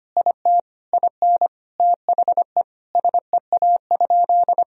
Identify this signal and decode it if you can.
Morse code, heard as a keyed tone, about 25 wpm, 700 Hz. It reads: IT IN THE SEA?
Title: Sentences from Top 1000 Words - Review Part 1 - 25wpm